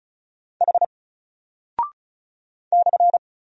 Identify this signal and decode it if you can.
Morse code, heard as a keyed tone, about 36 wpm, 700 Hz. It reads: H /
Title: Single Character (Letter:Number:Punctuation:Prosign) 35wpm